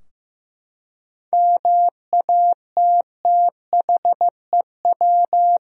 Morse code, heard as a keyed tone, about 15 wpm, 700 Hz. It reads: MATTHEW
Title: USA Common Names - 15wpm